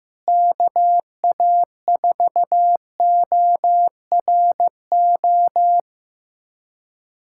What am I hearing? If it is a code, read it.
Morse code, heard as a keyed tone, about 15 wpm, 700 Hz. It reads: KA4ORO
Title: USA Callsign DE Callsign 15wpm